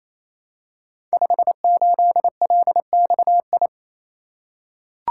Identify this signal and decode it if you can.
Morse code, heard as a keyed tone, about 27 wpm, 700 Hz. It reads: 58LXI
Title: Code Groups - 5 Letter:Numbers - 28wpm